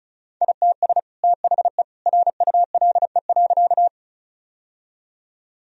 Morse code, heard as a keyed tone, about 36 wpm, 700 Hz. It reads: ITS THE RULE.
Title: Sentences from Top 1000 Words - Review Part 2 - 35wpm